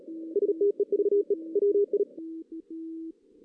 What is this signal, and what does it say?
Morse code, heard as a keyed tone, about 33 wpm, 380 Hz. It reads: STEVE WI
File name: 2016 JAN CW NAQP